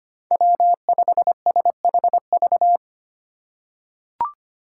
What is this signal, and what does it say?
Morse code, heard as a keyed tone, about 25 wpm, 700 Hz. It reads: W5SHV
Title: USA Call Signs - 5 Characters - 2x Character Spacing - 25wpm